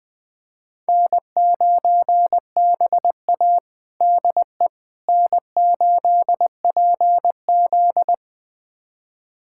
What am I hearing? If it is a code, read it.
Morse code, heard as a keyed tone, about 20 wpm, 700 Hz. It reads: N9BA DE N8PZ
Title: USA Callsign DE Callsign - 4 Characters - 20wpm